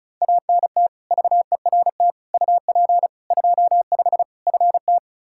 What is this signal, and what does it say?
Morse code, heard as a keyed tone, about 34 wpm, 700 Hz. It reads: ANT VERT UP 25 FT
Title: K1USN SST QSO Practice 35wpm